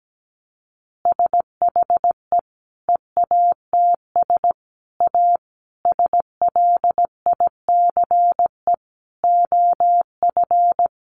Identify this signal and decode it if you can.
Morse code, heard as a keyed tone, about 17 wpm, 700 Hz. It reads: SHE EATS A SLICE OF
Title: Easy Phrases 17wpm